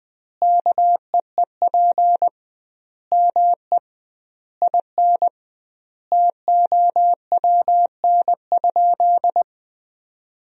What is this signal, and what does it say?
Morse code, heard as a keyed tone, about 21 wpm, 700 Hz. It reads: KEEP ME IN TOWN?
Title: 2x Word-Spacing - Sentences from Top 500 Words - 20wpm